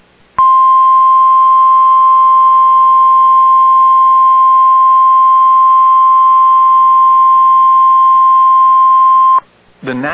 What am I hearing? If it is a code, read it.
Emergency Alert System (SAME) tones (the 1050 Hz attention tone). It is heard from 0.4 to 9.4 s.